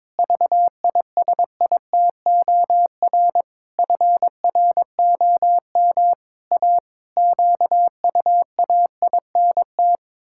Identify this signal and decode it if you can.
Morse code, heard as a keyed tone, about 22 wpm, 700 Hz. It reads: VISITOR FROM A QUAINT
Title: Pangram Challenge v2 - 22wpm